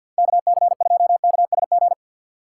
Morse code, heard as a keyed tone, about 52 wpm, 700 Hz. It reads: KC2KSG
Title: USA Call Signs - 2x Character Spacing - 50wpm